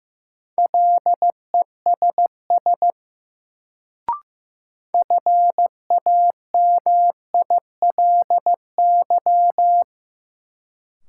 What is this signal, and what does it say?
Morse code, heard as a keyed tone, about 15 wpm, 700 Hz. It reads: LESS FAMILY
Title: Sentences from Top 400 Words - New Word Drill - 15wpm